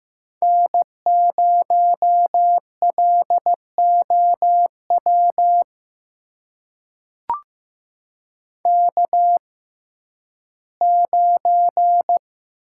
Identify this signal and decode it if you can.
Morse code, heard as a keyed tone, about 15 wpm, spaced as for 11 wpm, 700 Hz. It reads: N0LOW K9
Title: USA Call Signs - 5 Characters - 6x Character Spacing - 15wpm